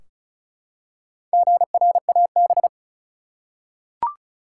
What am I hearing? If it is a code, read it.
Morse code, heard as a keyed tone, about 36 wpm, 700 Hz. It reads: GRAB
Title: Four Letter Words - 35wpm